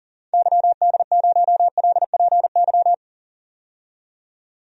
Morse code, heard as a keyed tone, about 40 wpm, 700 Hz. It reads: YD0LPY
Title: DXCC-100 Call Signs - 6 Characters - 40wpm